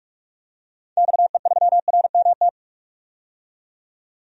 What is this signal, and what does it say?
Morse code, heard as a keyed tone, about 46 wpm, 700 Hz. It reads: XE3RMT
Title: DXCC-100 Call Signs - 6 Characters - 45wpm